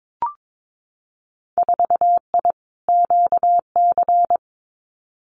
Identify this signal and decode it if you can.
Morse code, heard as a keyed tone, about 22 wpm, 700 Hz. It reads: 4I QC
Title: Winter Field Day Exchanges Encore 22wpm